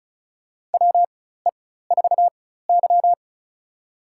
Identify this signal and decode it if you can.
Morse code, heard as a keyed tone, about 34 wpm, spaced as for 16 wpm, 700 Hz. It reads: WE4Y
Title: USA Call Signs - 4 Characters - 4x Character Spacing - 35wpm